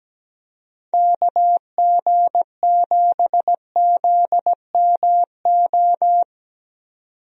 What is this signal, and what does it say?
Morse code, heard as a keyed tone, about 17 wpm, 700 Hz. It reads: KG7ZMO
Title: USA Call Signs - 6 Characters - 17wpm